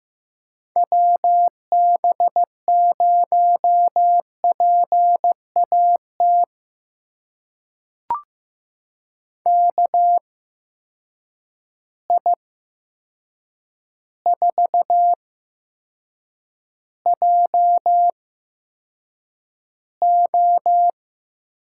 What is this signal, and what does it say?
Morse code, heard as a keyed tone, about 15 wpm, spaced as for 7 wpm, 700 Hz. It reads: WB0PAT KI4JO
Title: USA Call Signs - 6 Characters - 8x Character Spacing - 15wpm